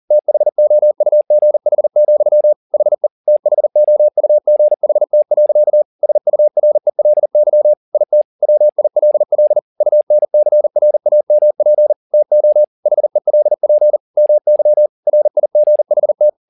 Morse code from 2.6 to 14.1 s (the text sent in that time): HE THOUGHT. SURELY IT WILL UNCRAMP TO HELP